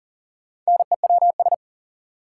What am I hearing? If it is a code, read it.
Morse code, heard as a keyed tone, about 40 wpm, 700 Hz. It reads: NEWS
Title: Sentences from Top 800 Words - New Word Drill - 40wpm